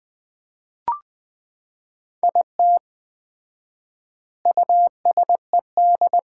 Morse code, heard as a keyed tone, about 20 wpm, 700 Hz. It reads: IT USED
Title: 4x Word-Spacing - Sets of 4 Words - 20wpm